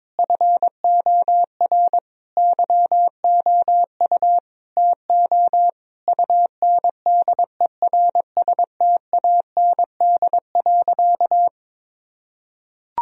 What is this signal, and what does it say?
Morse code, heard as a keyed tone, about 22 wpm, 700 Hz. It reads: FOR YOU TO UNDERSTAND.
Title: Sentences from Top 1000 Words - Review Part 3 - 22wpm